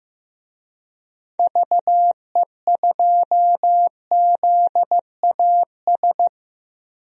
Morse code, heard as a keyed tone, about 15 wpm, 700 Hz. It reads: VE2ZAS
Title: DXCC-100 Call Signs - 6 Characters - Repeated 2x - 15wpm